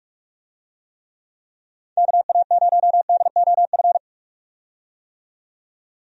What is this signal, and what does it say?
Morse code, heard as a keyed tone, about 46 wpm, 700 Hz. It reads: KA0DOF
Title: USA Call Signs - 6 Characters - 45wpm